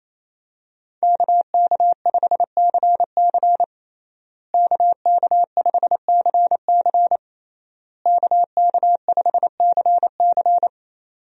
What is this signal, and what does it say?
Morse code, heard as a keyed tone, about 28 wpm, 700 Hz. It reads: KK5CC KK5CC KK5CC
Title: USA Call Signs - 5 Characters - Repeated 3x - 28wpm